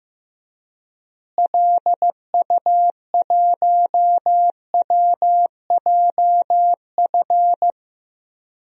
Morse code, heard as a keyed tone, about 15 wpm, 700 Hz. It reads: LU1WJF
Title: DXCC-100 Call Signs - 6 Characters - Repeated 2x - 15wpm